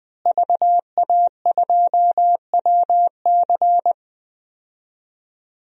Morse code, heard as a keyed tone, about 21 wpm, 700 Hz. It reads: VA2WC
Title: DXCC-100 Call Signs - 5 Characters - Repeated 3x - 20wpm